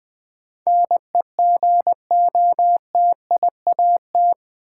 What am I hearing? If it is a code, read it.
Morse code, heard as a keyed tone, about 20 wpm, 700 Hz. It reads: NEGOTIAT
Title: Build A Word - 4 Letters At A Time - 11-Letter Words - 20wpm